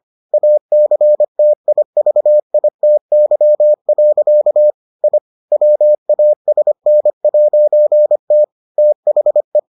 Morse code, heard as a keyed tone, about 25 wpm, 600 Hz. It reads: ACTIVITY. I WASN'T THE